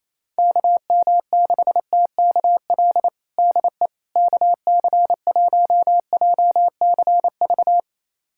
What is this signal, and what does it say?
Morse code, heard as a keyed tone, about 28 wpm, 700 Hz. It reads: KM6TKL DE KC1JCV
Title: USA Callsign DE Callsign 28wpm